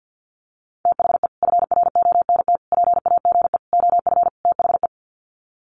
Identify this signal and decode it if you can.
Morse code, heard as a keyed tone, about 50 wpm, 700 Hz. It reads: THE FRONT PAGE OF THE
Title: Sets of 5 Words Encore 50wpm